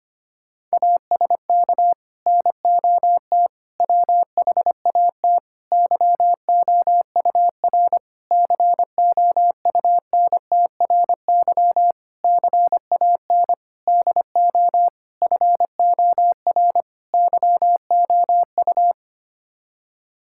Morse code, heard as a keyed tone, about 25 wpm, 700 Hz. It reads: ASK NOT WHAT YOUR COUNTRY CAN DO FOR YOU
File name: Sets of 9 Words - 25wpm